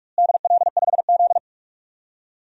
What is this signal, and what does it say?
Morse code, heard as a keyed tone, about 46 wpm, 700 Hz. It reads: DL5Z
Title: DXCC-100 Call Signs - 4 Characters - Repeated 2x - 45wpm